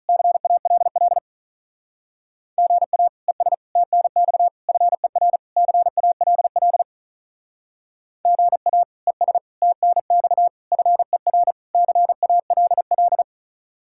Morse code, heard as a keyed tone, about 44 wpm, 700 Hz. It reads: CALL GA ES TNX FER CALL GA ES TNX FER CALL